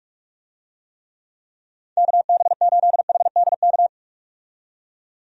Morse code, heard as a keyed tone, about 46 wpm, 700 Hz. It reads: KB8HDK
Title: USA Call Signs - 6 Characters - 45wpm